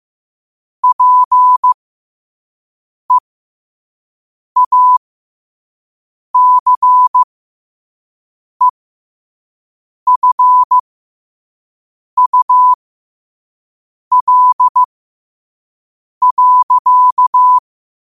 Morse code, heard as a keyed tone, about 15 wpm, spaced as for 5 wpm, 1000 Hz. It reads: PEACEFUL.
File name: qotdfiles / 5wpm